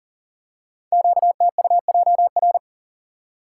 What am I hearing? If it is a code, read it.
Morse code, heard as a keyed tone, about 41 wpm, 700 Hz. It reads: QTUJR
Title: Code Groups - 5 Letters - 40wpm